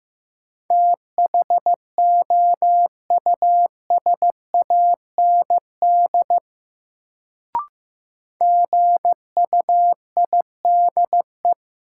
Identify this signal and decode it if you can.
Morse code, heard as a keyed tone, about 15 wpm, 700 Hz. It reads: THOUSAND GUIDE
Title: Sentences from Top 600 Words - New Word Drill - 15wpm